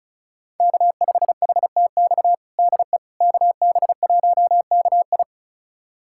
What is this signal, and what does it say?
Morse code, heard as a keyed tone, about 34 wpm, 690 Hz. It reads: K5HTX DE KB1KI
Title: USA Callsign DE Callsign - 5 Characters - 35wpm